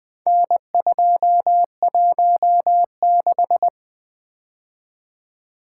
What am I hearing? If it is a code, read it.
Morse code, heard as a keyed tone, about 20 wpm, 700 Hz. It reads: N216
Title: Four Letter-Number Combinations - 20wpm